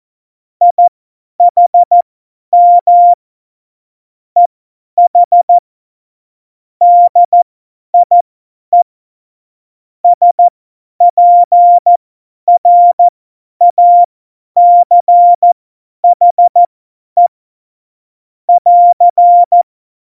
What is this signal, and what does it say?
Morse code, heard as a keyed tone, about 14 wpm, spaced as for 10 wpm, 700 Hz. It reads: IHM EH DIE SPRACHE <AR>
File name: Morsekurs / Witze / Witz 27. August 2019, 50 F70